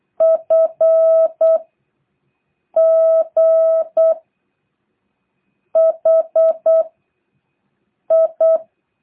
Morse code, heard as a keyed tone, about 8 wpm, 650 Hz. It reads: F G H I
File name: jh8cbh / 2023cw